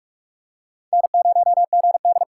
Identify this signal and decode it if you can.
Morse code, heard as a keyed tone, about 43 wpm, 700 Hz. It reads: N0GD